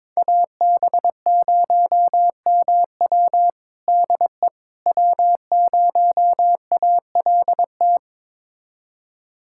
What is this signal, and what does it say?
Morse code, heard as a keyed tone, about 22 wpm, 700 Hz. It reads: AB0MW DE W0ALT
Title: USA Callsign DE Callsign - 5 Characters - 22wpm